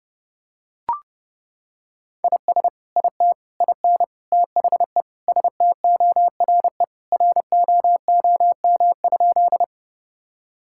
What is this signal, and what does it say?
Morse code, heard as a keyed tone, about 32 wpm, 700 Hz. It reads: IS IT IN THE STORE ROOM?
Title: Sentences from Top 1000 Words - Review Part 5 - 30wpm